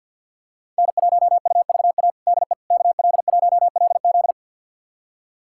Morse code, heard as a keyed tone, about 50 wpm, 700 Hz. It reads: N1UVA DE KL1LZ
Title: USA Callsign DE Callsign - 5 Characters - 50wpm